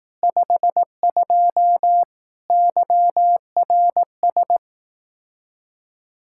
Morse code, heard as a keyed tone, about 18 wpm, 700 Hz. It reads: 52 YRS
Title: Speed-Racing Common QSO Phrases Encore - 15wpm